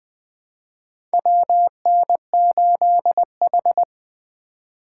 Morse code, heard as a keyed tone, about 21 wpm, 700 Hz. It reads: WN8H
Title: USA Call Signs - 4 Characters - 2x Character Spacing - 20wpm